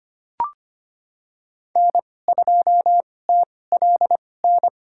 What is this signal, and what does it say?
Morse code, heard as a keyed tone, about 25 wpm, spaced as for 18 wpm, 700 Hz. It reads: N2TLN
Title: USA Call Signs - 2x Character Spacing - 25wpm